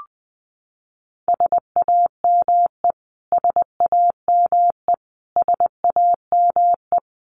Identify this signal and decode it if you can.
Morse code, heard as a keyed tone, about 20 wpm, 700 Hz. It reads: SAME SAME SAME
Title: Top 200 Words Repeated 20wpm